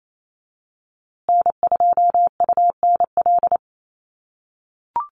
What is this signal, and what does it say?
Morse code, heard as a keyed tone, about 28 wpm, 700 Hz. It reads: N2UNL
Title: USA Call Signs Repeated 2x 28wpm